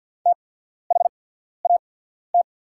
Morse code, heard as a keyed tone, about 48 wpm, spaced as for 14 wpm, 690 Hz. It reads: THAT
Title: 8x Character Spacing - Top 500 Words - 50wpm